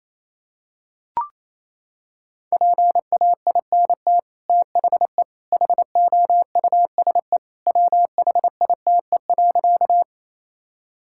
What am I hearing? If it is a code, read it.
Morse code, heard as a keyed tone, about 28 wpm, 700 Hz. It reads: PAINT THE HOUSE WHITE.
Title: Sentences from Top 1000 Words - Review Part 6 - 28wpm